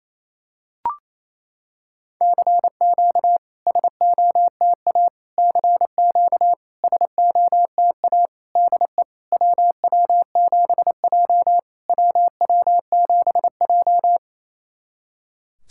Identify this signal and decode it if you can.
Morse code, heard as a keyed tone, about 28 wpm, 700 Hz. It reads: CQ SOTA CQ SOTA DE WW7J WW7J
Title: SOTA QSO Practice 28wpm